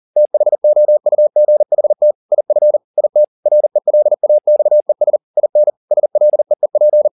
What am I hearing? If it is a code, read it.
Morse code, heard as a keyed tone, about 40 wpm, 600 Hz. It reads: THOUGHT IF IT RELAXES IN SLEEP
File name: the old man and the sea 2 40wpm